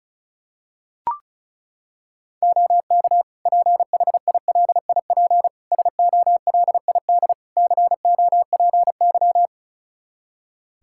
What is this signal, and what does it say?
Morse code, heard as a keyed tone, about 36 wpm, 700 Hz. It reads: OK PHILIP SOLID COPY
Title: K1USN SST QSO Practice 35wpm